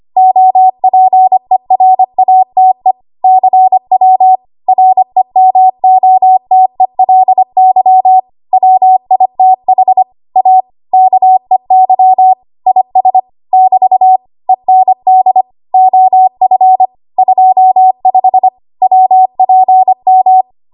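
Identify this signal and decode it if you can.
Morse code, heard as a keyed tone, about 25 wpm, 750 Hz. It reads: OPERATE CW REMOTELY WITH A KEY IS <BT> END OF 25 WPM
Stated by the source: ARRL 25 WPM code practice archive.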